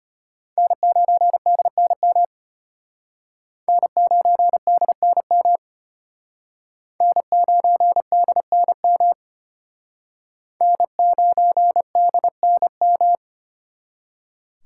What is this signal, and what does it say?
Morse code, heard as a keyed tone, about 32 wpm, 700 Hz. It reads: N9DNM N9DNM N9DNM N9DNM